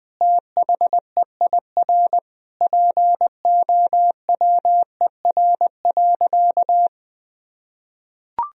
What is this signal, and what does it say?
Morse code, heard as a keyed tone, about 20 wpm, 700 Hz. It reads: THEIR POWER.
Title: Sentences from Top 1000 Words - Review Part 2 - 20wpm